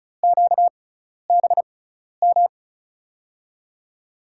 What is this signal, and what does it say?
Morse code, heard as a keyed tone, about 36 wpm, spaced as for 12 wpm, 700 Hz. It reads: QBM
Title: USA Call Signs - 5 Characters - 6x Character Spacing - 35wpm